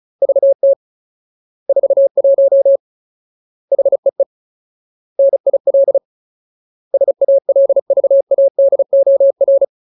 Morse code, heard as a keyed tone, about 34 wpm, 550 Hz. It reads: VT 41 HEE NIL SALVADOR
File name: Ultimate Word List / 35WPM Ultimate